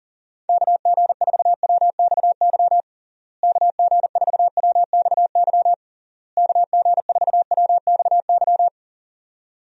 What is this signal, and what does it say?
Morse code, heard as a keyed tone, about 40 wpm, 700 Hz. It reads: KG4WXY KG4WXY KG4WXY
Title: USA Call Signs - 6 Characters - Repeated 3x - 40wpm